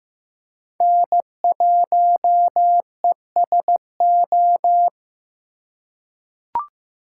Morse code, heard as a keyed tone, about 15 wpm, 700 Hz. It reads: N1ESO
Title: USA Call Signs - 5 Characters - 4x Character Spacing - 15wpm